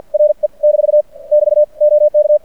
Morse code, heard as a keyed tone, about 50 wpm, 590 Hz. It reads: RE - <BT> OK
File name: archived / CRSnet